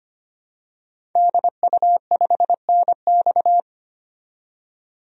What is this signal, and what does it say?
Morse code, heard as a keyed tone, about 25 wpm, 700 Hz. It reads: DU5NX
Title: DXCC-100 Call Signs - 5 Characters - 25wpm